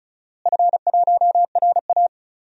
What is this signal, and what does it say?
Morse code, heard as a keyed tone, about 34 wpm, 700 Hz. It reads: F1RA
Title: DXCC-100 Call Signs - 4 Characters - Repeated 2x - 35wpm